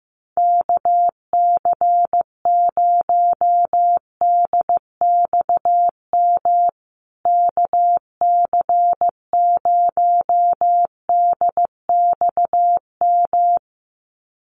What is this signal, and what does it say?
Morse code, heard as a keyed tone, about 16 wpm, 700 Hz. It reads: KC0DXM KC0DXM
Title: USA Call Signs Repeated 2x 15wpm